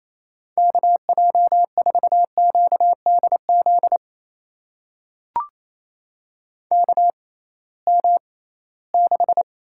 Morse code, heard as a keyed tone, about 28 wpm, spaced as for 18 wpm, 700 Hz. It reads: KJ4QDZ KM6
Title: USA Call Signs - 6 Characters - 6x Character Spacing - 28wpm